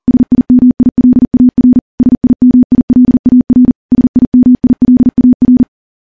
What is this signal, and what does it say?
Morse code, heard as a keyed tone, about 40 wpm, 250 Hz. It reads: SIMILAR SIMILAR SIMILAR
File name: morse / morse-training-40WPM